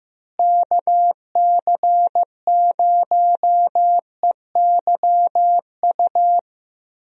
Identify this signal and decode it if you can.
Morse code, heard as a keyed tone, about 15 wpm, 700 Hz. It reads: KC0EYU
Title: USA Call Signs - 6 Characters - Repeated 3x - 15wpm